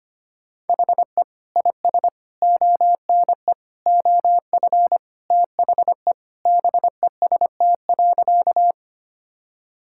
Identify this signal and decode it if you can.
Morse code, heard as a keyed tone, about 25 wpm, 700 Hz. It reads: HE IS ONE OF THE BEST.
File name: practice / 25wpm / Sentences with 4-Word Phrases - 25wpm